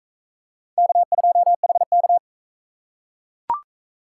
Morse code, heard as a keyed tone, about 40 wpm, 700 Hz. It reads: K2HK